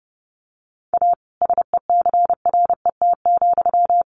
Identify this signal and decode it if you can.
Morse code, heard as a keyed tone, about 30 wpm, 700 Hz. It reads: A SECRET,
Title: Sentences from Top 1200 Words 30wpm